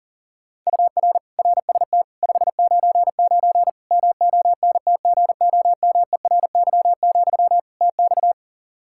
Morse code, heard as a keyed tone, about 37 wpm, 700 Hz. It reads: UR RST 599 MONTGOMERY, TX
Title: K1USN SST QSO Practice 40wpm